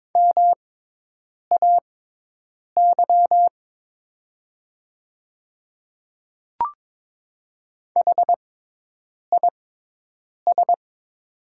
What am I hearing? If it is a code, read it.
Morse code, heard as a keyed tone, about 22 wpm, spaced as for 8 wpm, 700 Hz. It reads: MAY HIS
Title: 6x Character Spacing - Top 100 Words - 22wpm